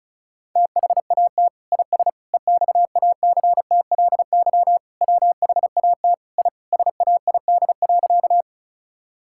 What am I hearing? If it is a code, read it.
Morse code, heard as a keyed tone, about 34 wpm, 700 Hz. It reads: THAT IS EXACTLY WHAT I SAID.
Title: Sentences from Top 1000 Words - Review Part 6 - 35wpm